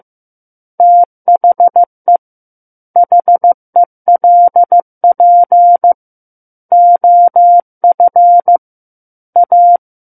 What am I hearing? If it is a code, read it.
Morse code, heard as a keyed tone, about 15 wpm, 700 Hz. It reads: THE HELP OF A
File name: Book One / 06-Chapter 6 - 15 WPM